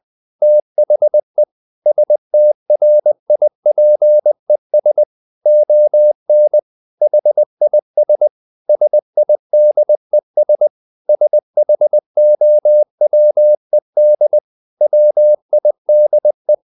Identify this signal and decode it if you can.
Morse code, heard as a keyed tone, about 20 wpm, 600 Hz. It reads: THE STRIPES ON HIS SIDES SHOWED WIDE
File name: the old man and the sea / the old man and the sea part 1 20wpm